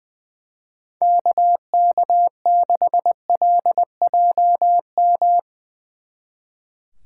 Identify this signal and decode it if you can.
Morse code, heard as a keyed tone, about 20 wpm, 700 Hz. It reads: KK6LJM
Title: Speed Racing - USA 2x3 Call Signs 20wpm